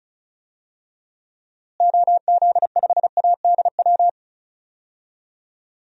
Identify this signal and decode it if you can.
Morse code, heard as a keyed tone, about 36 wpm, 700 Hz. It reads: OZ5ADW
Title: DXCC-100 Call Signs - 6 Characters - Repeated 3x - 35wpm